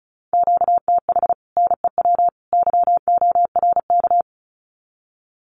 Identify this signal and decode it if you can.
Morse code, heard as a keyed tone, about 34 wpm, 700 Hz. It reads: QTH NEW YORK
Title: Common QSO Phrases 35wpm